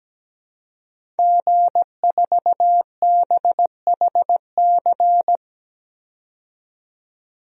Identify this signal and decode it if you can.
Morse code, heard as a keyed tone, about 17 wpm, 700 Hz. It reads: G4BHC
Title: DXCC-100 Call Signs - 5 Characters - 17wpm